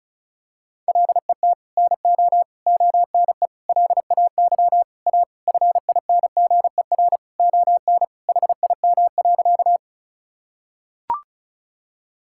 Morse code, heard as a keyed tone, about 36 wpm, 700 Hz. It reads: LET NO ONE LAY A FINGER ON HIM.
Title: Sentences from Top 1600 Words 35wpm